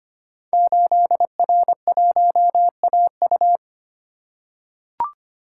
Morse code, heard as a keyed tone, about 25 wpm, 700 Hz. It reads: 8R1AU
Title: Code Groups - 5 Letter:Numbers - 25wpm